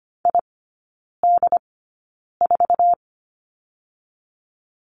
Morse code, heard as a keyed tone, about 25 wpm, spaced as for 9 wpm, 700 Hz. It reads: ID4
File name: Lesson 15e - Number 5 - Three Characters 25wpm